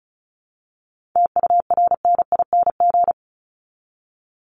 Morse code, heard as a keyed tone, about 34 wpm, 700 Hz. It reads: TURNING
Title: ING Suffix - Top 100 Words with Base Words 35wpm